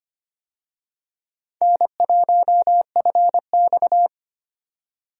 Morse code, heard as a keyed tone, about 25 wpm, 700 Hz. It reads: N1FX
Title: USA Call Signs - 4 Characters - 25wpm